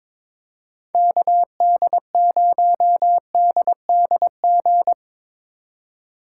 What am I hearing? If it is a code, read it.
Morse code, heard as a keyed tone, about 25 wpm, 700 Hz. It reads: KD0DDG
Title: Speed Racing - USA 2x3 Call Signs 22wpm